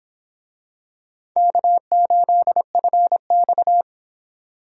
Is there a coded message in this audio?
Yes: Morse code from 1.4 to 3.8 s, heard as a keyed tone, about 25 wpm, 700 Hz. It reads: K8FX